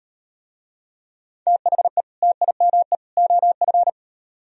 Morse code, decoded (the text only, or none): THE TIME OF